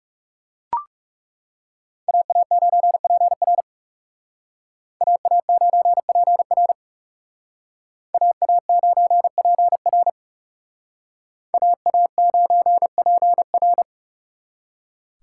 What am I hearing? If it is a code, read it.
Morse code, heard as a keyed tone, about 39 wpm, 700 Hz. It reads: AA9PR AA9PR AA9PR AA9PR